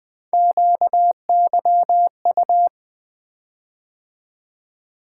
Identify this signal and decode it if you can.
Morse code, heard as a keyed tone, about 20 wpm, 700 Hz. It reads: QYU
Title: Three Letter-Number Combinations - 20wpm